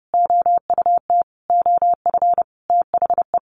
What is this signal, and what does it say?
Morse code, heard as a keyed tone, about 30 wpm, 700 Hz. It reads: OUT OF THE
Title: Sentences from Top 400 Words 30wpm